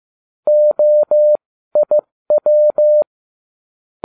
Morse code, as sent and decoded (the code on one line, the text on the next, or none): --- .. .--
OIW